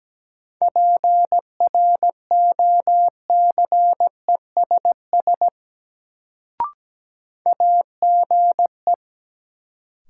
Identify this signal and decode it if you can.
Morse code, heard as a keyed tone, about 17 wpm, 700 Hz. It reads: PROCESS AGE
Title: Sets of 3 Words - New Word Drill - 17wpm